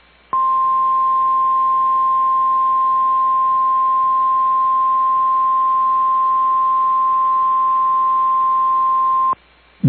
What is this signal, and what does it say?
Emergency Alert System (SAME) tones (the 1050 Hz attention tone). It is heard from 0.3 to 9.4 s.
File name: EAS Logs / KMNR